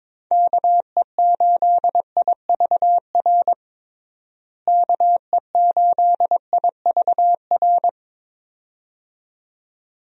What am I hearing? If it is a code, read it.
Morse code, heard as a keyed tone, about 22 wpm, 700 Hz. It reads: KE8IVR KE8IVR
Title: USA Call Signs - 6 Characters - Repeated 2x - 22wpm